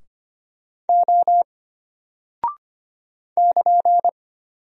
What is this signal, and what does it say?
Morse code, heard as a keyed tone, about 25 wpm, 700 Hz. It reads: O <KN>
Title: Single Character (Letter:Number:Punctuation:Prosign) Alt 25wpm